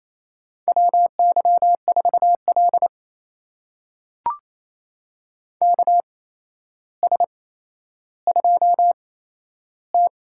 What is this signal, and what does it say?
Morse code, heard as a keyed tone, about 28 wpm, spaced as for 12 wpm, 700 Hz. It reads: WY4L KS2T
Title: USA Call Signs - 4 Characters - 8x Character Spacing - 28wpm